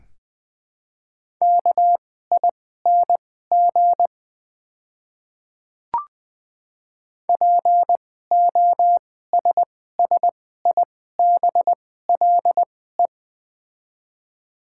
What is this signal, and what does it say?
Morse code, heard as a keyed tone, about 20 wpm, spaced as for 14 wpm, 700 Hz. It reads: KING POSSIBLE
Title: 2x Character Spacing - Top 500 Words - 20wpm